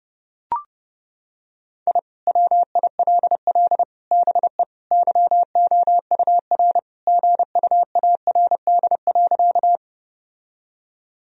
Morse code, heard as a keyed tone, about 31 wpm, 700 Hz. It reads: I WILL BE YOUR GUARD.
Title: Sentences from Top 1000 Words - Review Part 4 - 30wpm